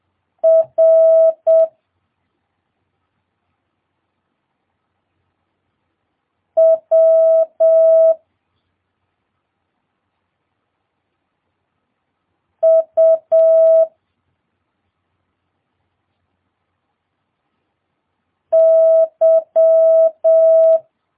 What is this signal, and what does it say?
Morse code, heard as a keyed tone, about 7 wpm, 650 Hz. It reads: R W U Y
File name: jh8cbh / 2022cw